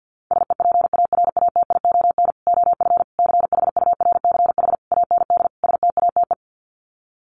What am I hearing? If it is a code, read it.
Morse code, heard as a keyed tone, about 50 wpm, 700 Hz. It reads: SEPARATION OF CHURCH AND STATE
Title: Sets of 5 Words 50wpm